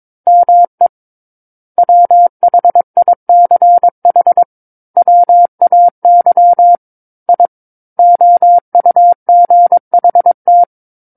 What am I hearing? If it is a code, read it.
Morse code, heard as a keyed tone, about 22 wpm, 700 Hz. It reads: ME WHICH WAY I OUGHT
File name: 06-Chapter-06 22 WPM